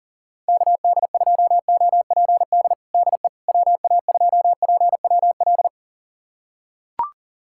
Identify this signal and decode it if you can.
Morse code, heard as a keyed tone, about 40 wpm, 700 Hz. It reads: KD2OPD DE WA2PWL
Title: USA Callsign DE Callsign 40wpm